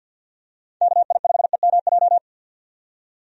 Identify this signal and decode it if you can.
Morse code, heard as a keyed tone, about 51 wpm, 700 Hz. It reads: KI5EMJ
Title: USA Call Signs - 6 Characters - 50wpm